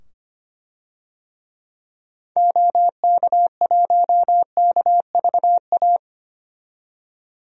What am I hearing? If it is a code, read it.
Morse code, heard as a keyed tone, about 25 wpm, 700 Hz. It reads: OK1KVA
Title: DXCC-100 Call Signs - 6 Characters - 25wpm